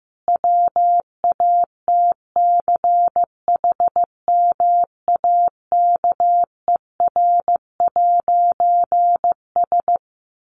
Morse code, heard as a keyed tone, about 15 wpm, 700 Hz. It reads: WATCHMAKER'S
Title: A Sound Of Thunder 15wpm